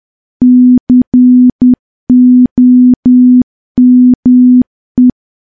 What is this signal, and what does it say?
Morse code, heard as a keyed tone, about 10 wpm, 250 Hz. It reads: COME
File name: morse / morse-training-10WPM